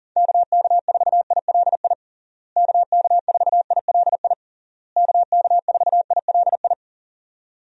Morse code, heard as a keyed tone, about 37 wpm, 700 Hz. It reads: KK4ILI KK4ILI KK4ILI
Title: USA Call Signs - 6 Characters - Repeated 3x - 40wpm